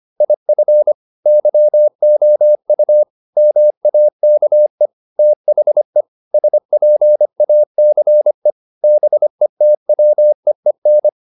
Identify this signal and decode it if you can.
Morse code, heard as a keyed tone, about 25 wpm, 600 Hz. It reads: IF YOU MAKE THE SPACE BETWEEN